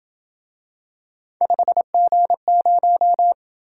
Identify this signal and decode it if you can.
Morse code, heard as a keyed tone, about 26 wpm, 700 Hz. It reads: 5G0